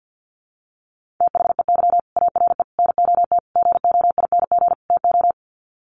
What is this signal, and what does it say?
Morse code, heard as a keyed tone, about 48 wpm, 700 Hz. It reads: THEY ARE NOT GOING TO
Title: Sets of 5 Words Encore2 50wpm